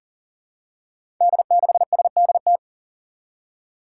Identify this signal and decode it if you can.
Morse code, heard as a keyed tone, about 40 wpm, 700 Hz. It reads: D6SDT
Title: Five Challenging Letter-Number Combinations 40wpm